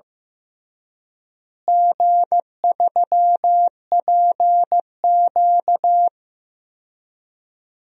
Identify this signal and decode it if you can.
Morse code, heard as a keyed tone, about 15 wpm, 700 Hz. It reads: G3PQ
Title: DXCC-100 Call Signs - 4 Characters - Repeated 3x - 15wpm